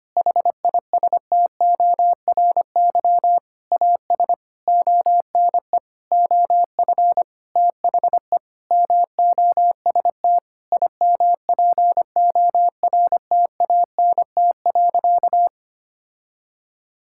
Morse code, heard as a keyed tone, about 25 wpm, 700 Hz. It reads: HISTORY AS ONE OF THE MOST IMPORTANT.
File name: practice / 25wpm / Sentences with 9-Word Phrases - 25wpm